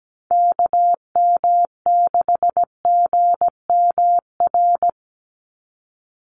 Morse code, heard as a keyed tone, about 17 wpm, 700 Hz. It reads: KM6GMR
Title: USA Call Signs 17wpm